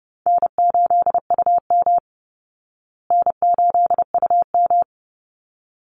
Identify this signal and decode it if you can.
Morse code, heard as a keyed tone, about 28 wpm, 700 Hz. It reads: N8UM N8UM
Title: Call Signs Repeated 3x - 30wpm